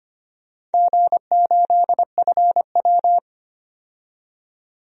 Morse code, heard as a keyed tone, about 25 wpm, 700 Hz. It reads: G8FW
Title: DXCC-100 Call Signs - 4 Characters - 25wpm